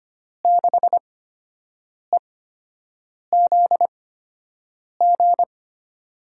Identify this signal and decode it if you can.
Morse code, heard as a keyed tone, about 25 wpm, spaced as for 7 wpm, 700 Hz. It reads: 6EZG
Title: USA Call Signs - 6 Characters - 8x Character Spacing - 25wpm